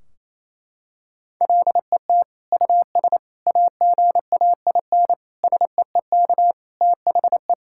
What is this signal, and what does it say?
Morse code, heard as a keyed tone, about 28 wpm, 700 Hz. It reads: LET US AGAIN SEEK THE
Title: Sentences from Top 1600 Words 28wpm